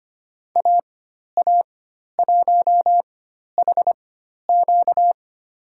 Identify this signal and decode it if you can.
Morse code, heard as a keyed tone, about 25 wpm, spaced as for 12 wpm, 700 Hz. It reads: AA1HQ
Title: USA Call Signs - 5 Characters - 4x Character Spacing - 25wpm